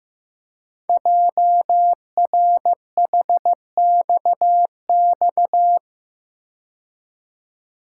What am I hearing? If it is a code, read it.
Morse code, heard as a keyed tone, about 15 wpm, 690 Hz. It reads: JRHXX
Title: Code Groups - 5 Letters - 15wpm